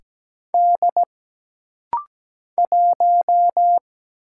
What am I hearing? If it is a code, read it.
Morse code, heard as a keyed tone, about 17 wpm, 700 Hz. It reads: D 1
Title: Single Letter-Number - Rapid-Fire 17wpm